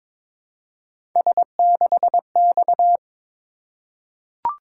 Morse code, heard as a keyed tone, about 22 wpm, 700 Hz. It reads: S6X
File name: practice / 22wpm / Three Letter-Number Combinations - 22wpm